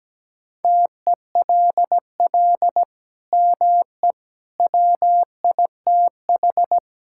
Morse code, heard as a keyed tone, about 17 wpm, 700 Hz. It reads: TELL ME WITH
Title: Sentences from Top 1000 Words - Review Part 5 - 17wpm